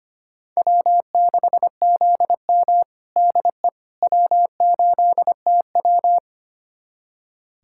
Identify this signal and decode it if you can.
Morse code, heard as a keyed tone, about 25 wpm, 700 Hz. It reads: W6ZM DE W8TW
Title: USA Callsign DE Callsign - 4 Characters - 25wpm